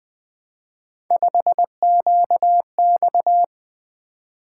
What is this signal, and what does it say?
Morse code, heard as a keyed tone, about 20 wpm, 700 Hz. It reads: 5QX
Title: Three Letter-Number Combinations - 20wpm